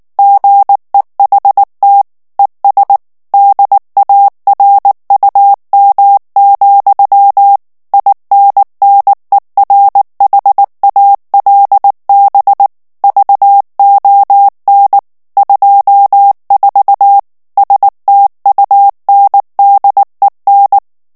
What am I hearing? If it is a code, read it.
Morse code, heard as a keyed tone, about 19 wpm, 800 Hz. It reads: GEHT ES DARUM, INNERHALB VON 24 STUNDEN
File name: morse mp3 / ALLEMAND texte clair 2 vitesse 18 mots minute